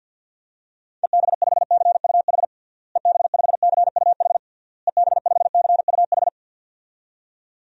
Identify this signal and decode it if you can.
Morse code, heard as a keyed tone, about 53 wpm, 700 Hz. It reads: EB5CUH EB5CUH EB5CUH
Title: DXCC-100 Call Signs - 6 Characters - Repeated 3x - 50wpm